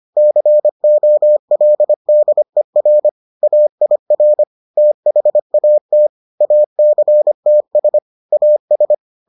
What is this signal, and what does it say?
Morse code, heard as a keyed tone, about 25 wpm, 600 Hz. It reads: COLDER AIR THAT ACTS AS